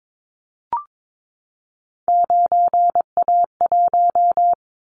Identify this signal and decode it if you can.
Morse code, heard as a keyed tone, about 22 wpm, 700 Hz. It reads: 9A1
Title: Callsign Prefixes for Top 50 DXCC - 22wpm